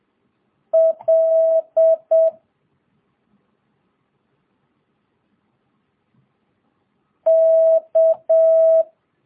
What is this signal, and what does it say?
Morse code, heard as a keyed tone, about 7 wpm, 650 Hz. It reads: L K
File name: jh8cbh / 2022cw